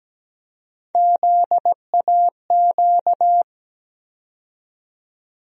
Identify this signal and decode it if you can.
Morse code, heard as a keyed tone, about 17 wpm, 700 Hz. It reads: ZAQ
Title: USA Call Sign Postfixes - 3 Letters 17wpm